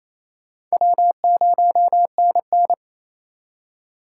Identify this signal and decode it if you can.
Morse code, heard as a keyed tone, about 36 wpm, 700 Hz. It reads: W0NN